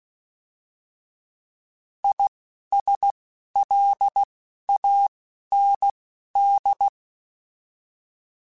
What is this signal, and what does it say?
Morse code, heard as a keyed tone, about 16 wpm, spaced as for 11 wpm, 800 Hz. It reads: ISLAND
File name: Alphabetical / Alphabetical mp3s / Is @10